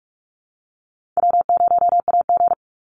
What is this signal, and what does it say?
Morse code, heard as a keyed tone, about 44 wpm, 700 Hz. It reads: W0AG
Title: Call Signs - 45wpm